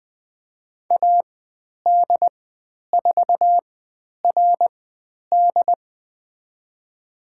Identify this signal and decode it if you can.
Morse code, heard as a keyed tone, about 20 wpm, spaced as for 10 wpm, 700 Hz. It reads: AD4RD
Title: 095 Lesson 16g - Letter c - Call Signs 20wpm